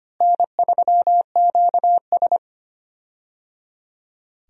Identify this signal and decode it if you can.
Morse code, heard as a keyed tone, about 25 wpm, 700 Hz. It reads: N3QS
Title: USA Call Signs - 4 Characters - Repeated 3x - 25wpm